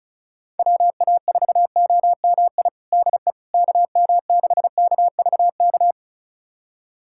Morse code, heard as a keyed tone, about 36 wpm, 690 Hz. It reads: WA4OMI DE KM6KVK
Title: USA Callsign DE Callsign - 6 Characters - 35wpm